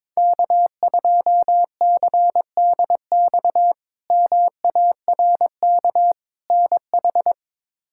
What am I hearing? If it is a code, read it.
Morse code, heard as a keyed tone, about 22 wpm, 700 Hz. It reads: K2CDX MARK NH
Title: K1USN SST QSO Practice - 22wpm